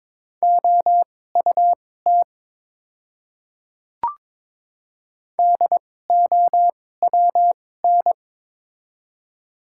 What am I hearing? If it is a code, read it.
Morse code, heard as a keyed tone, about 22 wpm, spaced as for 16 wpm, 700 Hz. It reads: OUT DOWN
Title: 2x Character Spacing - Top 100 Words - 22wpm